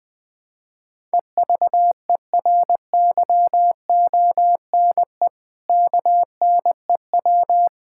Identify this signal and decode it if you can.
Morse code, heard as a keyed tone, about 21 wpm, 700 Hz. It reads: EVERYONE KNEW
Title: Sentences from Top 1000 Words - Review Part 5 - 20wpm